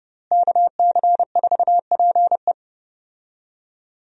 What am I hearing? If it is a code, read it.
Morse code, heard as a keyed tone, about 30 wpm, 700 Hz. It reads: KC4PE
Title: USA Call Signs - 5 Characters - Repeated 2x - 30wpm